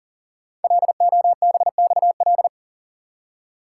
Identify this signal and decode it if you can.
Morse code, heard as a keyed tone, about 42 wpm, 700 Hz. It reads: LOBXL
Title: Code Groups - 5 Letters - 40wpm